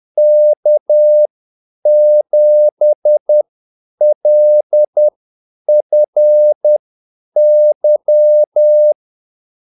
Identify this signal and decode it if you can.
Morse code, heard as a keyed tone, about 10 wpm, spaced as for 8 wpm, 600 Hz. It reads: K7LFY
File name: Callsigns-10-8wpm